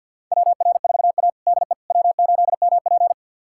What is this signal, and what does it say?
Morse code, heard as a keyed tone, about 53 wpm, 690 Hz. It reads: WR4A DE W8MP
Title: USA Callsign DE Callsign - 4 Characters - 50wpm